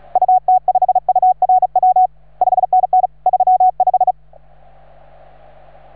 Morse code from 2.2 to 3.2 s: 5NN